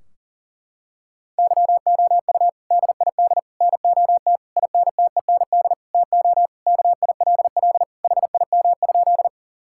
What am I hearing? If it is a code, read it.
Morse code, heard as a keyed tone, about 40 wpm, 700 Hz. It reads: YOU DID NOT INTEND TO KILL HIM?
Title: Sentences from Top 1600 Words 40wpm